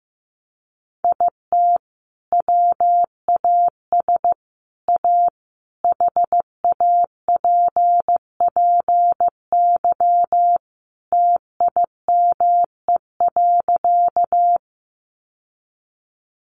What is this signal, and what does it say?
Morse code, heard as a keyed tone, about 15 wpm, 700 Hz. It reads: IT WAS A HAPPY TIME.
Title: Sentences from Top 1000 Words 15wpm